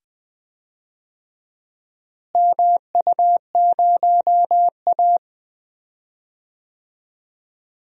Morse code, heard as a keyed tone, about 20 wpm, 700 Hz. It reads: MU0A
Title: DXCC-100 Call Signs - 4 Characters - Repeated 2x - 20wpm